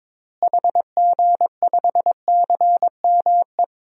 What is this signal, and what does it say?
Morse code, heard as a keyed tone, about 22 wpm, 700 Hz. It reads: HG5CME
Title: DXCC-100 Call Signs - 6 Characters - Repeated 3x - 22wpm